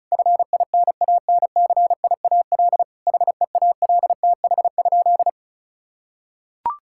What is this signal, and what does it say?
Morse code, heard as a keyed tone, about 36 wpm, 700 Hz. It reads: FINANCIAL HEALTH?
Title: Sentences from Top 1000 Words - Review Part 5 - 35wpm